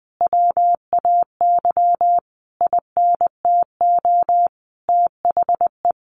Morse code, heard as a keyed tone, about 20 wpm, 700 Hz. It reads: WAY INTO THE
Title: Sentences from Top 1300 Words 20wpm